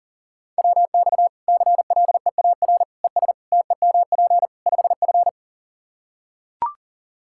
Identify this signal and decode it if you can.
Morse code, heard as a keyed tone, about 37 wpm, 700 Hz. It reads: WX CLEAR ES TEMP 5F
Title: Practice QSO - Exchange 2 - Info - 40wpm